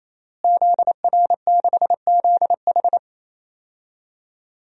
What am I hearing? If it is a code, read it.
Morse code, heard as a keyed tone, about 27 wpm, 700 Hz. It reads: ZR6ZH
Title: DXCC-100 Call Signs - 5 Characters - Repeated 3x - 28wpm